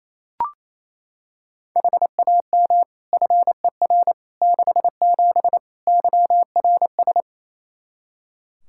Morse code, heard as a keyed tone, about 28 wpm, 700 Hz. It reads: HAM FER 67 YRS
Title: Practice QSO - Exchange 3 - More Info - 28wpm